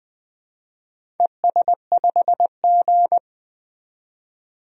Morse code, heard as a keyed tone, about 20 wpm, 700 Hz. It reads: ES5G
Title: DXCC-100 Call Signs - 4 Characters - 20wpm